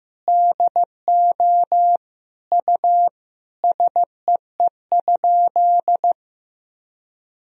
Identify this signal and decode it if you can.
Morse code, heard as a keyed tone, about 16 wpm, 700 Hz. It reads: DO U SEE?
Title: Sentences from Top 100 Words and Abbreviations 15wpm